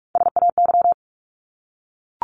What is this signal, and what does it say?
Morse code, heard as a keyed tone, about 44 wpm, 700 Hz. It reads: SAY
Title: Top 200 Words 45wpm